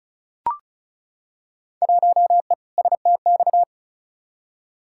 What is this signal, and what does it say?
Morse code, heard as a keyed tone, about 36 wpm, 700 Hz. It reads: 1E STX
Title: Field Day Exchanges - 35wpm